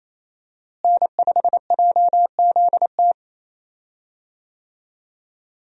Morse code, heard as a keyed tone, about 28 wpm, 700 Hz. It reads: N5JZT
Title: USA Call Signs - 5 Characters - Repeated 2x - 28wpm